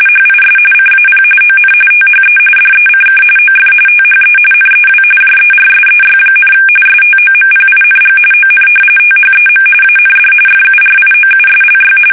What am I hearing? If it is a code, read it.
RTTY, about 45.45 baud. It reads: HIS CAP ON THE DECK OF THE NEW JERSEY, STOMPING ON THEM AND FURIOUSLY
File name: fsk850 / battleshipnj